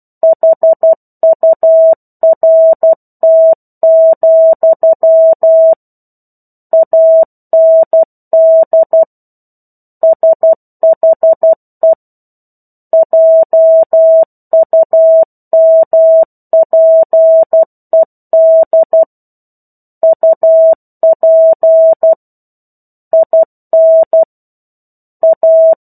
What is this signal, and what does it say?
Morse code, heard as a keyed tone, about 12 wpm, 650 Hz. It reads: HURT, AND SHE JUMPED UP IN A
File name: 01-Chapter-01 12 WPM